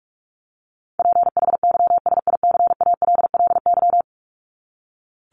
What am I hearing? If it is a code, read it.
Morse code, heard as a keyed tone, about 44 wpm, 700 Hz. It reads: PHYSICALLY
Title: Top 1501-1750 Words 45wpm